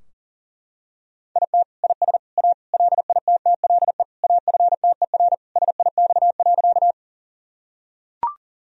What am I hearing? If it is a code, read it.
Morse code, heard as a keyed tone, about 37 wpm, 700 Hz. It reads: IT IS A LITTLE AFTER SIX.
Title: Sentences from Top 1000 Words - Review Part 2 - 40wpm